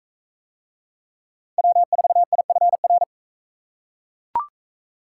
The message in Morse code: W4IFR

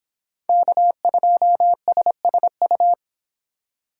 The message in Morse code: K2SSU

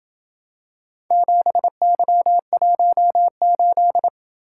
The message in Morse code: 7Y18

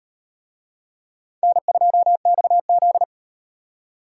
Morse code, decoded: N2XZ

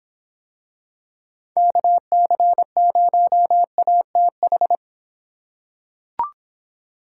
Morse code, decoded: KC0ATH

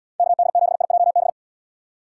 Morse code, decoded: KD7IJB

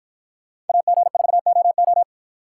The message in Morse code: AG4OO

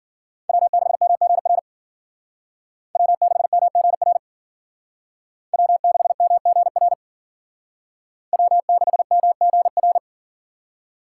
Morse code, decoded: W6MGR W6MGR W6MGR W6MGR

